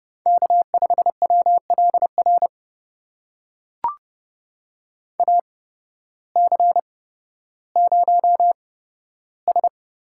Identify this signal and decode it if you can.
Morse code, heard as a keyed tone, about 30 wpm, spaced as for 14 wpm, 700 Hz. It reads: K5WLR AC0S
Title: USA Call Signs - 5 Characters - 8x Character Spacing - 30wpm